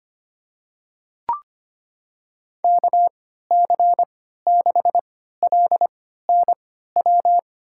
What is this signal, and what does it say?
Morse code, heard as a keyed tone, about 25 wpm, spaced as for 14 wpm, 700 Hz. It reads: KC6LNW
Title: USA Call Signs - 6 Characters - 3x Character Spacing - 25wpm